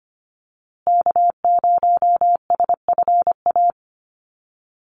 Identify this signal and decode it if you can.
Morse code, heard as a keyed tone, about 25 wpm, 700 Hz. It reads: K0SFA
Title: USA Call Signs Repeated 2x 25wpm